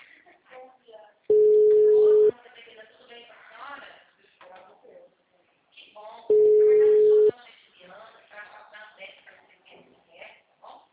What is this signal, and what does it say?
Ringback tone (Europe, 425 Hz), with speech.